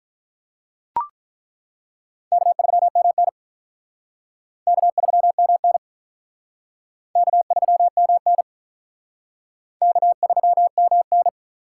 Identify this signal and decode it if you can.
Morse code, heard as a keyed tone, about 43 wpm, 700 Hz. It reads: K3MN K3MN K3MN K3MN
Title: Speed Racing - USA 1x2 Call Signs 35wpm